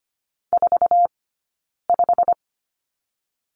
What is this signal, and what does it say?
Morse code, heard as a keyed tone, about 25 wpm, 700 Hz. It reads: 4 5
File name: course / 25:10wpm / 092 Lesson 16d - Letter c - Two Characters 25wpm